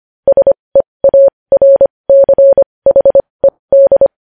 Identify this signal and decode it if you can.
Morse code, heard as a keyed tone, about 25 wpm, spaced as for 20 wpm, 550 Hz. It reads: SEARCHED
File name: podcast / News headlines-1x-25x20